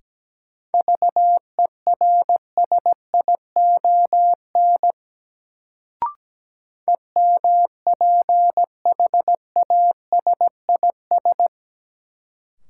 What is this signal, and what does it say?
Morse code, heard as a keyed tone, about 17 wpm, 700 Hz. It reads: VERSION EMPHASIS
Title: Sentences from Top 1400 Words - New Word Drill Alt - 17wpm